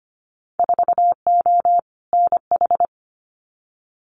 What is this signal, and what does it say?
Morse code, heard as a keyed tone, about 25 wpm, 700 Hz. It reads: 4O NH
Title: Winter Field Day Exchanges Encore 25wpm